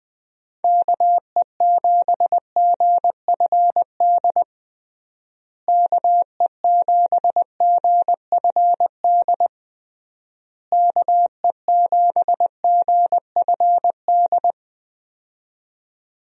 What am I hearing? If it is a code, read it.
Morse code, heard as a keyed tone, about 20 wpm, 700 Hz. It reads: KE7GFD KE7GFD KE7GFD
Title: USA Call Signs - 6 Characters - Repeated 3x - 20wpm